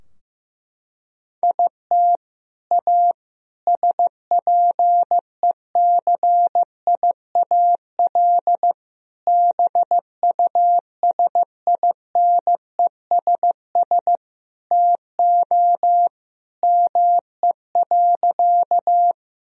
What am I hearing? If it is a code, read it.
Morse code, heard as a keyed tone, about 15 wpm, 700 Hz. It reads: IT A SPECIAL BUSINESS TO ME.
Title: Sentences from Top 1000 Words - Review Part 3 - 15wpm